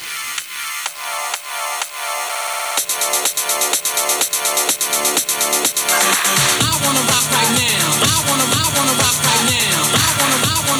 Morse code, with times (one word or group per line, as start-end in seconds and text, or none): none